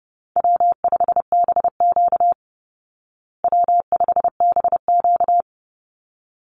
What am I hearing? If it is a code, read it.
Morse code, heard as a keyed tone, about 30 wpm, 700 Hz. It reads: W5BQ W5BQ
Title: Easy Call Signs Repeated 2x - 30wpm